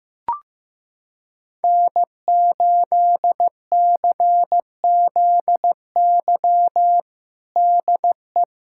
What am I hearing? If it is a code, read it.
Morse code, heard as a keyed tone, about 15 wpm, 700 Hz. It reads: N8CZY DE
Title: USA Callsign DE Callsign - 5 Characters - 15wpm